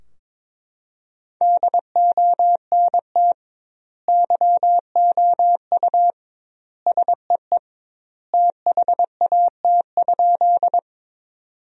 Morse code, heard as a keyed tone, about 22 wpm, 700 Hz. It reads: DONT YOU SEE THAT?
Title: 2x Word-Spacing - Sentences from Top 300 Words - 22wpm